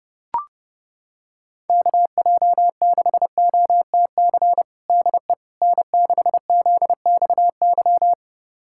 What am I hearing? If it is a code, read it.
Morse code, heard as a keyed tone, about 30 wpm, 690 Hz. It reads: KJ6OTC DE N6ZXY
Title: USA Callsign DE Callsign 30wpm